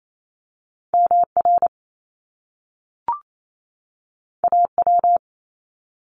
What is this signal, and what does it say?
Morse code, heard as a keyed tone, about 28 wpm, 700 Hz. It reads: MR AW
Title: Top Two Letter Combinations in Words 28wpm